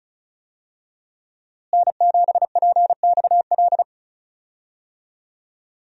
Morse code, heard as a keyed tone, about 36 wpm, 700 Hz. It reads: N7PXL
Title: USA Call Signs - 5 Characters - Repeated 3x - 35wpm